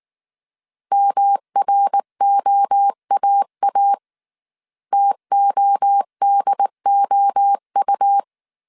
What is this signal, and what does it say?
Morse code, heard as a keyed tone, about 19 wpm, 800 Hz. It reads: MROAA TODOU